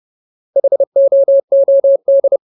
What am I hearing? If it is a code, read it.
Morse code, heard as a keyed tone, about 30 wpm, 550 Hz. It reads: HOOD
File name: Ultimate Word List / 30WPM Ultimate